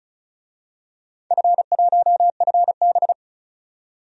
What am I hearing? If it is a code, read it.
Morse code, heard as a keyed tone, about 34 wpm, 700 Hz. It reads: F1FB
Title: DXCC-100 Call Signs - 4 Characters - 35wpm